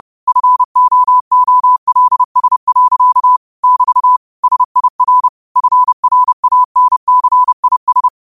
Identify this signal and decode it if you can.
Morse code, heard as a keyed tone, about 30 wpm, 1000 Hz. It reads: FOOLS. <BT> SIR FRANCIS